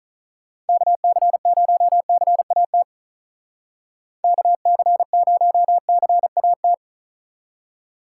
Morse code, heard as a keyed tone, about 43 wpm, 700 Hz. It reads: KC0CAT KC0CAT
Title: Speed Racing - USA 2x3 Call Signs 35wpm